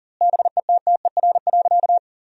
Morse code, heard as a keyed tone, about 40 wpm, 700 Hz. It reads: BETTER.
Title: 2x Word-Spacing - Sentences from Top 400 Words - 40wpm